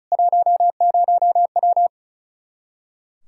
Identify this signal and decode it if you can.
Morse code, heard as a keyed tone, about 34 wpm, 700 Hz. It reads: 10W
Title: Practice QSO - Exchange 2 - Info - 35wpm